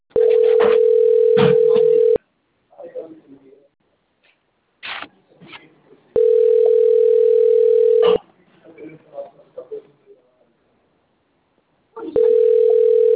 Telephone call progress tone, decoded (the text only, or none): ringback tone